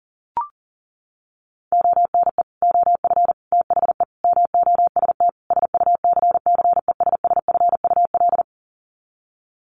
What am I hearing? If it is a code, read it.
Morse code, heard as a keyed tone, about 40 wpm, 700 Hz. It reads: ONE OF THE MOST SUCCESSFUL
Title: Sets of 5 Words Encore2 40wpm